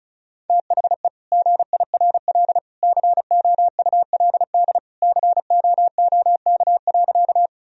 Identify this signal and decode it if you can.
Morse code, heard as a keyed tone, about 36 wpm, 700 Hz. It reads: THE GIRL COULD COOK.
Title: Sentences from Top 1000 Words - Review Part 4 - 35wpm